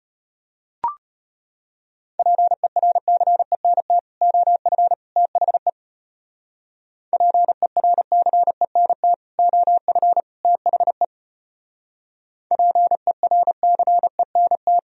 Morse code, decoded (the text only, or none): PERCENT OF THE PERCENT OF THE PERCENT